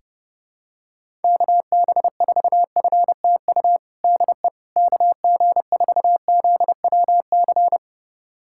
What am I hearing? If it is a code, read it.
Morse code, heard as a keyed tone, about 30 wpm, 700 Hz. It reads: KB4FTU DE KG4ZWC
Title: USA Callsign DE Callsign - 6 Characters - 30wpm